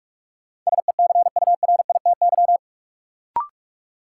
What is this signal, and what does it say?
Morse code, heard as a keyed tone, about 46 wpm, 700 Hz. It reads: SECURITY
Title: Sentences from Top 1000 Words - Review Part 4 - New Word Drill - 45wpm